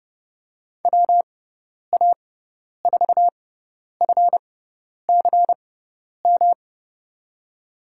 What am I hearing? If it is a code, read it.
Morse code, heard as a keyed tone, about 30 wpm, spaced as for 10 wpm, 700 Hz. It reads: WA4FCM
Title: USA Call Signs - 6x Character Spacing - 30wpm